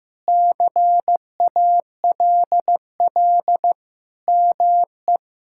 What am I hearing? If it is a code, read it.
Morse code, heard as a keyed tone, about 15 wpm, 700 Hz. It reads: CALL ME
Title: Sentences from Top 1600 Words 15wpm